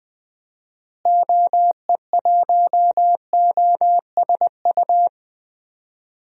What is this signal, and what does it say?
Morse code, heard as a keyed tone, about 21 wpm, 700 Hz. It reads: OE1OSU
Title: DXCC-100 Call Signs - 6 Characters - Repeated 2x - 20wpm